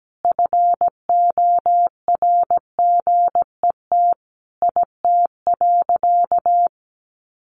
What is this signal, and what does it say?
Morse code, heard as a keyed tone, about 17 wpm, 700 Hz. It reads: FORGET IT.
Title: Sentences from Top 1200 Words 17wpm